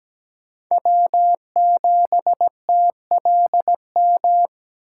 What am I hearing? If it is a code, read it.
Morse code, heard as a keyed tone, about 17 wpm, 700 Hz. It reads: W7TLM
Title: USA Call Signs - 5 Characters - 8x Character Spacing - 17wpm